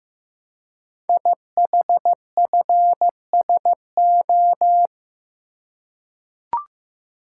Morse code, heard as a keyed tone, about 15 wpm, 700 Hz. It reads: IHFSO
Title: Code Groups - 5 Letters - 15wpm